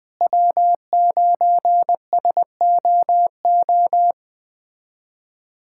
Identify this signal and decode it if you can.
Morse code, heard as a keyed tone, about 20 wpm, 700 Hz. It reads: W9SOO
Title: USA Callsign DE Callsign - 5 Characters - 20wpm